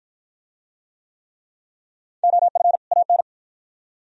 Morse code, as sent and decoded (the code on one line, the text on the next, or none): --- ..-. / .- -.
OF AN